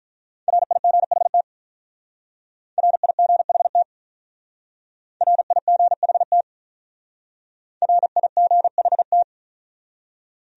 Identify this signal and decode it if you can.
Morse code, heard as a keyed tone, about 44 wpm, 700 Hz. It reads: RIGHT RIGHT RIGHT RIGHT